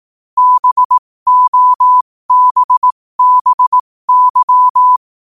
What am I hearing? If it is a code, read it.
Morse code, heard as a keyed tone, about 18 wpm, spaced as for 15 wpm, 1000 Hz. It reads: BOBBY
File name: qotdfiles / 15wpm